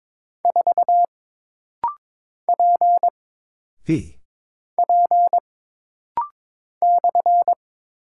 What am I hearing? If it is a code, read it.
Morse code, heard as a keyed tone, about 22 wpm, 700 Hz. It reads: 4 P P /
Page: This practice set is composed of Letters, Numbers, common punctuations, and common prosigns.